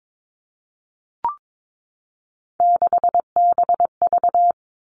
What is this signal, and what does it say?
Morse code, heard as a keyed tone, about 22 wpm, 700 Hz. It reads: 6BV
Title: Three Challenging Letter-Number Combinations 22wpm